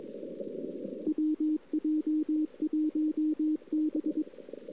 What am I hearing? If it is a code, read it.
Morse code, heard as a keyed tone, about 22 wpm, 310 Hz, with radio noise. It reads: WJ1B
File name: CW / 2008-ss-cw-k5tr-op-n5rz -02:00